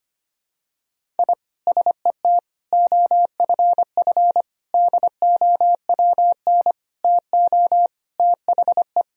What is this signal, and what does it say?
Morse code, heard as a keyed tone, about 25 wpm, 700 Hz. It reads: I SET OFF DOWN TO THE
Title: Sentences from Top 1000 Words - Review Part 1 - 25wpm